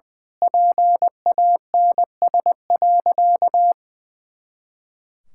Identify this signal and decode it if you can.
Morse code, heard as a keyed tone, about 21 wpm, 700 Hz. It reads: PANS.
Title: Pangram Challenge v2 - 20wpm